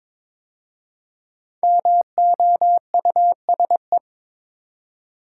Morse code, heard as a keyed tone, about 22 wpm, 700 Hz. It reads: MOUSE
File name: 5 Letter Words - First 3 Letters Given - 22wpm